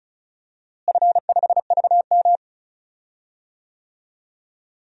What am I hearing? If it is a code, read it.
Morse code, heard as a keyed tone, about 34 wpm, 700 Hz. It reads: F5VM
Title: DXCC-100 Call Signs - 4 Characters - Repeated 2x - 35wpm